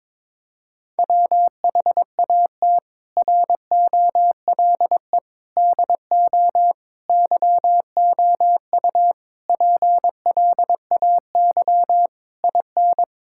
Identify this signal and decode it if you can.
Morse code, heard as a keyed tone, about 22 wpm, 700 Hz. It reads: WHAT ROLE DO YOU PLAY IN
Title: Sentences from Top 1000 Words - Review Part 6 - 22wpm